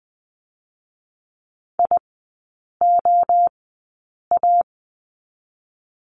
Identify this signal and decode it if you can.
Morse code, heard as a keyed tone, about 20 wpm, 700 Hz. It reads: I O A
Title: 029 Lesson 6e - Letter i - Three Characters 20wpm